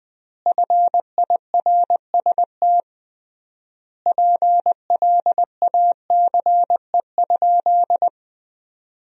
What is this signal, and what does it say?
Morse code, heard as a keyed tone, about 20 wpm, 700 Hz. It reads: FIRST PLACE?
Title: 3x Word-Spacing - Sentences from Top 200 Words - 20wpm